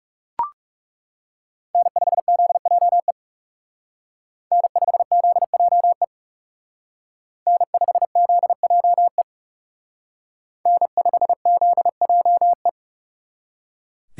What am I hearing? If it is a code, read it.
Morse code, heard as a keyed tone, about 36 wpm, 700 Hz. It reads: N5ZJE N5ZJE N5ZJE N5ZJE
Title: Speed Racing - USA 1x3 Call Signs 30wpm